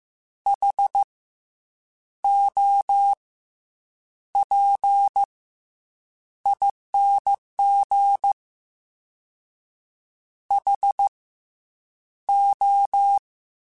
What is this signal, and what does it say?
Morse code, heard as a keyed tone, about 15 wpm, spaced as for 7 wpm, 800 Hz. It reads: HOPING HO
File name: Misc / Misc-mp3s / Jumps @13